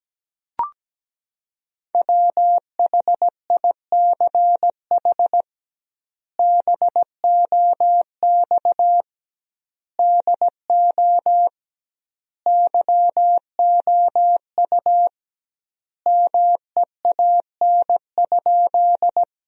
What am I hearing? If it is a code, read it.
Morse code, heard as a keyed tone, about 17 wpm, 700 Hz. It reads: WHICH BOX DO YOU MEAN?
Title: 2x Word-Spacing - Sentences from Top 500 Words - 17wpm